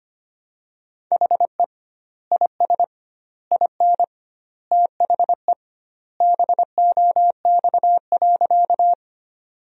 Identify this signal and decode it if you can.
Morse code, heard as a keyed tone, about 25 wpm, 700 Hz. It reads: HE IS IN THE BOX.
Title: 2x Word-Spacing - Sentences from Top 500 Words - 25wpm